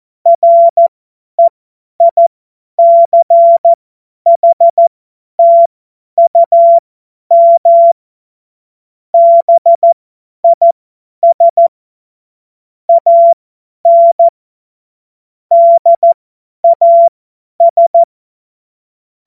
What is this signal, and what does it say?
Morse code, heard as a keyed tone, about 14 wpm, spaced as for 10 wpm, 680 Hz. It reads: REICHTUM BIS AN DAS
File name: Morsekurs / Witze / Witz 27 Februar 2024 50 F70